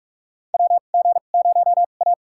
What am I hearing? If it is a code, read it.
Morse code, heard as a keyed tone, about 44 wpm, spaced as for 32 wpm, 700 Hz. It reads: WG0A
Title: USA Call Signs - 4 Characters - 2x Character Spacing - 45wpm